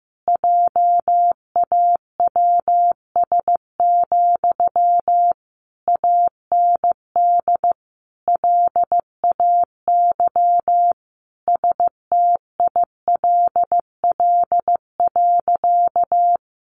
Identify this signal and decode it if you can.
Morse code, heard as a keyed tone, about 15 wpm, 700 Hz. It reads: JAWS, AND LAY STILL.
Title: A Sound Of Thunder 15wpm